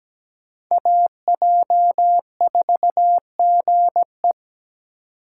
Morse code, heard as a keyed tone, about 17 wpm, 700 Hz. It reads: AJ4GE
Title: USA Call Signs - 2x Character Spacing - 17wpm